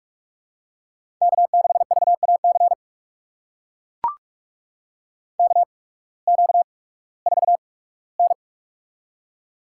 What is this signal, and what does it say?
Morse code, heard as a keyed tone, about 43 wpm, spaced as for 21 wpm, 700 Hz. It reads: K6VAC KQ4N
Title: USA Call Signs - 8x Character Spacing - 45wpm